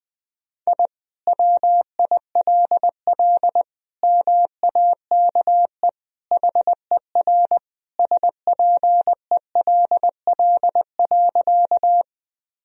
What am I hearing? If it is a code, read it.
Morse code, heard as a keyed tone, about 20 wpm, 700 Hz. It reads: I WILL MAKE HER SPELL.
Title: Sentences from Top 1000 Words - Review Part 1 - 20wpm